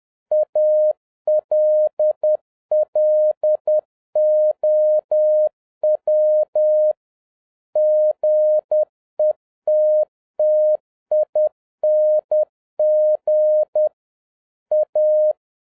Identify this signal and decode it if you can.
Morse code, heard as a keyed tone, about 10 wpm, 610 Hz. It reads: ALLOW GETTING A